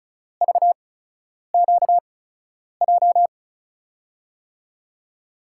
Morse code, heard as a keyed tone, about 36 wpm, spaced as for 10 wpm, 700 Hz. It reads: VQJ
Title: USA Call Signs - 8x Character Spacing - 35wpm